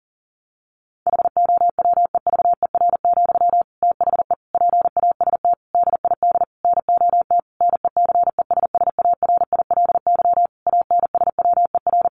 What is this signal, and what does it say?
Morse code, heard as a keyed tone, about 40 wpm, 700 Hz. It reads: HOWEVER, THE PAST DID NOT NECESSARILY ANSWER
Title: Sentences from Top 1400 Words 40wpm